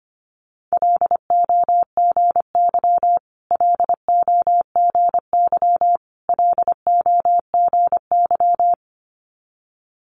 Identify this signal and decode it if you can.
Morse code, heard as a keyed tone, about 25 wpm, 700 Hz. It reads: LOGY LOGY LOGY
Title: Multi-letter Phonemes Repeated 3x 25wpm